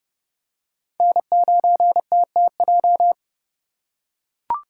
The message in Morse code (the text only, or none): N9TTJ